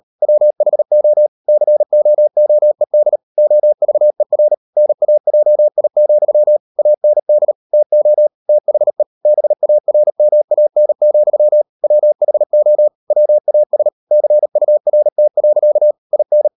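Morse code, heard as a keyed tone, about 38 wpm, 600 Hz. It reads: WHO COOED OVER NAJI, AND TO THE BARMAN, WHO WAS CURT. IN
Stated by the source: Morse code practice at 38 WPM, sending Ken Follett's novel Never in English, from CW med Gnister.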